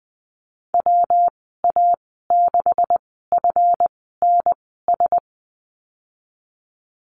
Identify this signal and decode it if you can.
Morse code, heard as a keyed tone, about 20 wpm, spaced as for 15 wpm, 700 Hz. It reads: WA6FNS
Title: 154 Lesson 25g - Letter p - Call Signs 20wpm